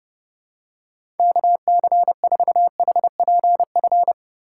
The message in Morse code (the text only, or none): KC4HPF